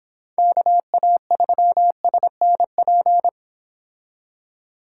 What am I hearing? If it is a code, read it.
Morse code, heard as a keyed tone, about 26 wpm, 700 Hz. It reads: KA3SNP